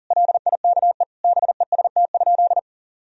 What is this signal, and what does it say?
Morse code, heard as a keyed tone, about 40 wpm, 700 Hz. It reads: LIKE BEST?
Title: Sentences from Top 1000 Words - Review Part 2 - 40wpm